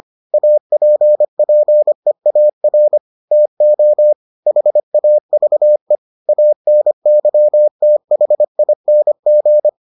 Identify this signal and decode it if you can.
Morse code, heard as a keyed tone, about 25 wpm, 600 Hz. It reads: APPEAR TO HAVE ANYTHING